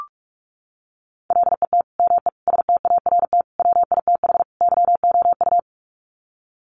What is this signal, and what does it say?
Morse code, heard as a keyed tone, about 44 wpm, 700 Hz. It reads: LET ME START WITH YOU
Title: Sets of 5 Words Encore 45wpm